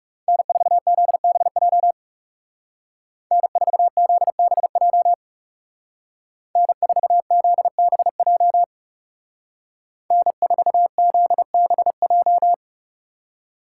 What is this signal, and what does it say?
Morse code, heard as a keyed tone, about 35 wpm, 700 Hz. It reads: N4ZBJ N4ZBJ N4ZBJ N4ZBJ
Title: Speed Racing - USA 1x3 Call Signs 30wpm